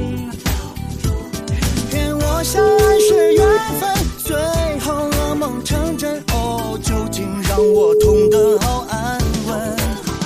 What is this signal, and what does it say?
Ringback tone (Europe, 425 Hz), with noise.